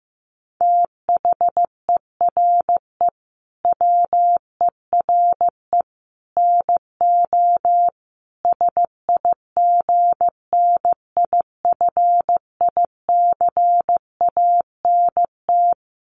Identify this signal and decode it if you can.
Morse code, heard as a keyed tone, about 15 wpm, 700 Hz. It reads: THERE WERE NO SIGNIFICANT
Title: Sets of 5 Words Encore2 15wpm